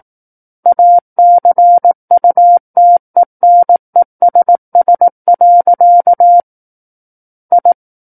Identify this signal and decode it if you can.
Morse code, heard as a keyed tone, about 18 wpm, 700 Hz. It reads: ACUTENESS. I
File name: Book One / 09-Chapter 9 - 18 WPM